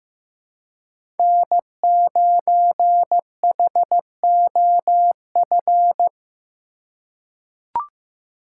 Morse code, heard as a keyed tone, about 16 wpm, 700 Hz. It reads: N9HOF
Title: USA Call Signs - 5 Characters - 3x Character Spacing - 15wpm